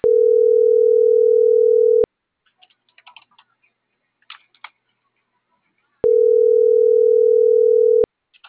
Ringback tone (North America), 440 + 480 Hz.